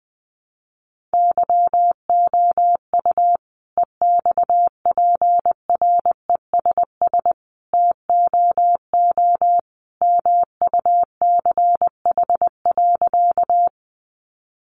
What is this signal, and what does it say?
Morse code, heard as a keyed tone, about 20 wpm, 700 Hz. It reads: YOU EXPRESS TOO MUCH.
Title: Sentences from Top 1100 Words 20wpm